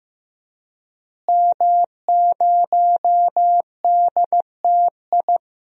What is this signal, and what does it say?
Morse code, heard as a keyed tone, about 15 wpm, 700 Hz. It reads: M0DTI
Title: DXCC-100 Call Signs - 5 Characters - Repeated 3x - 15wpm